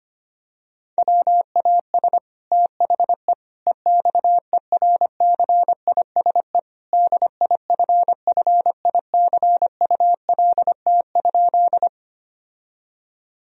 Morse code, heard as a keyed tone, about 25 wpm, 700 Hz. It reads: WAS THE EXERCISE DIFFICULT?
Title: Sentences from Top 1000 Words - Review Part 4 - 25wpm